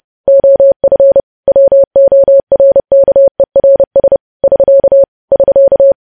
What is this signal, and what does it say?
Morse code, heard as a keyed tone, about 30 wpm, 550 Hz. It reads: OF WORKERS <SK> <SK>
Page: news headlines in morse code at 30 wpm